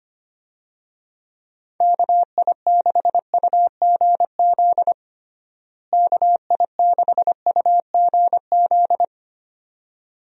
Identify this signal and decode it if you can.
Morse code, heard as a keyed tone, about 25 wpm, 700 Hz. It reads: KI6UGZ KI6UGZ
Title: USA Call Signs - 6 Characters - Repeated 2x - 25wpm